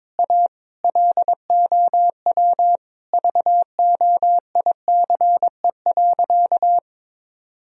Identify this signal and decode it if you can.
Morse code, heard as a keyed tone, about 22 wpm, 700 Hz. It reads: A LOW VOICE.
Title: Sentences from Top 1000 Words - Review Part 1 - 22wpm